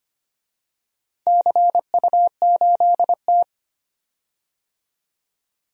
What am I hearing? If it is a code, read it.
Morse code, heard as a keyed tone, about 25 wpm, 700 Hz. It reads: CU8T
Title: DXCC-100 Call Signs - 4 Characters - 25wpm